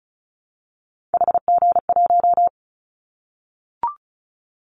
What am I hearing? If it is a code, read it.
Morse code, heard as a keyed tone, about 36 wpm, 700 Hz. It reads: HG1
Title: Callsign Prefixes for Top 50 DXCC - 35wpm